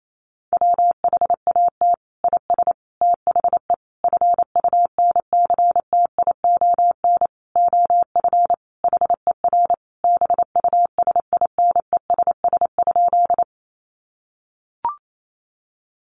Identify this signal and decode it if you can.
Morse code, heard as a keyed tone, about 28 wpm, 700 Hz. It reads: WHAT IS THE FUNCTION OF HER BUSINESS?
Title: Sentences from Top 1100 Words 28wpm